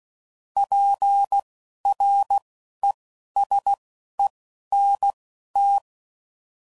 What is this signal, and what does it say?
Morse code, heard as a keyed tone, about 16 wpm, spaced as for 11 wpm, 800 Hz. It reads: PRESENT
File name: CW Code Groups @10WPM / Technical Subjects / Gates Intro